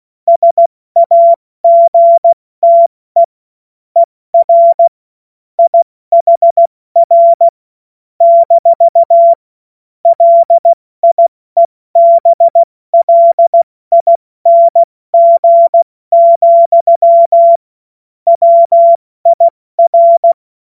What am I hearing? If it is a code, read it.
Morse code, heard as a keyed tone, about 16 wpm, 680 Hz. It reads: SAGTE ER IHR - LIEBLING, WIR
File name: Morsekurs / Witze / Witz 12 März 2024 70 F80